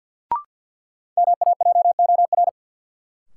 Morse code, heard as a keyed tone, about 51 wpm, 700 Hz. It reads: MAJOR
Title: Sentences from Top 600 Words - New Word Drill - 50wpm